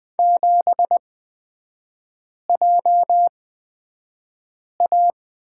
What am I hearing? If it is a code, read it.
Morse code, heard as a keyed tone, about 20 wpm, spaced as for 5 wpm, 700 Hz. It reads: 7JA
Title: 255 Lesson 41e - Period - Three Characters 20wpm